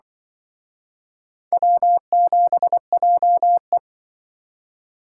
Morse code, heard as a keyed tone, about 25 wpm, 700 Hz. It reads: W7JE
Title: Speed Racing - USA 1x2 Call Signs 20wpm